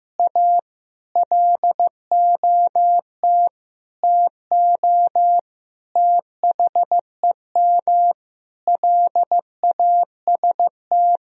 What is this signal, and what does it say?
Morse code, heard as a keyed tone, about 15 wpm, 700 Hz. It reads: A LOT TO THEM LAST
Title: Sentences from Top 1000 Words - Review Part 3 - 15wpm